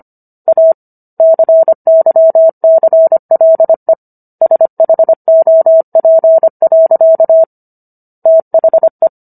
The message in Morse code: A CYCLE SHOP. THE

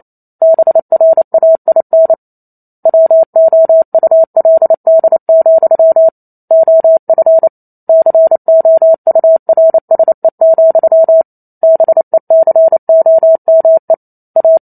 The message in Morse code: BRAIN WOULD, OF COURSE, BECOME A

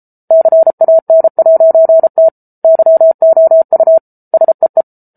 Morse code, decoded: CAN'T YOU SEE

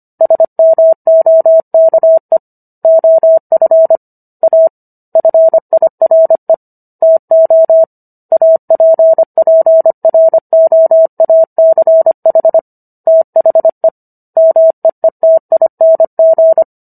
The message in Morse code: SMOKE OF A FIRE TO APPROACH THE MEETING

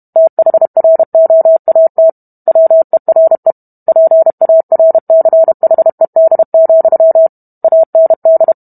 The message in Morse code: THROAT WERE PARCHED, AND

